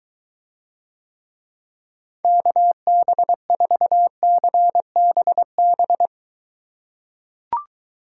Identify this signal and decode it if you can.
Morse code, heard as a keyed tone, about 23 wpm, 700 Hz. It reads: KB4CBB